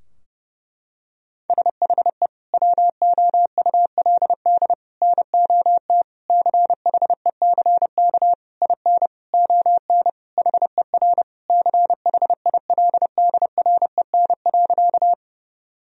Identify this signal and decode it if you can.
Morse code, heard as a keyed tone, about 32 wpm, 700 Hz. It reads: SHE WOULD NOT CHECK IN ON HER CHILDREN.
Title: Sentences from Top 1000 Words - Review Part 3 - 30wpm